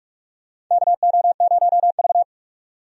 Morse code, heard as a keyed tone, about 46 wpm, 700 Hz. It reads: KO0V
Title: USA Call Signs - 4 Characters - 45wpm